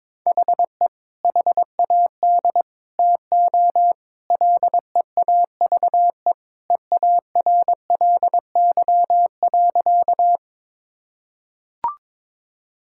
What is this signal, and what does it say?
Morse code, heard as a keyed tone, about 22 wpm, 700 Hz. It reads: HE HAD TO LEAVE EARLY.
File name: practice / 22wpm / Sentences with 2-Word Phrases - 22wpm